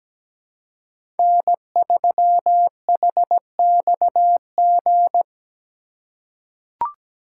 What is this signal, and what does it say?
Morse code, heard as a keyed tone, about 17 wpm, 700 Hz. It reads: N3HXG
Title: USA Call Signs - 5 Characters - 3x Character Spacing - 17wpm